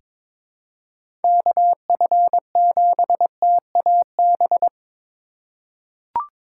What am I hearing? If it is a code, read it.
Morse code, heard as a keyed tone, about 22 wpm, 700 Hz. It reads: KF7TAB
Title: USA Call Signs - 4x Character Spacing - 22wpm